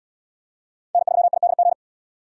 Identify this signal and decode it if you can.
Morse code, heard as a keyed tone, about 77 wpm, 700 Hz. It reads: N3IDG